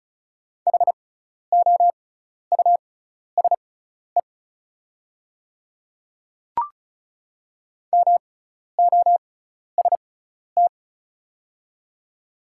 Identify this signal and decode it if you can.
Morse code, heard as a keyed tone, about 34 wpm, spaced as for 12 wpm, 700 Hz. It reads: HOUSE MOST
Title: 6x Character Spacing - Top 200 Words - 35wpm